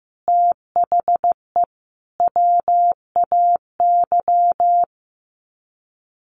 Morse code, heard as a keyed tone, about 15 wpm, 700 Hz. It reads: THE WAY
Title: Sets of 5 Words Encore 15wpm